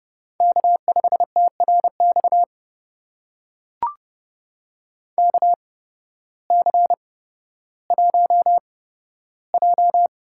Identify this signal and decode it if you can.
Morse code, heard as a keyed tone, about 29 wpm, spaced as for 14 wpm, 700 Hz. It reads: K5TRX KC1J
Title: USA Call Signs - 8x Character Spacing - 30wpm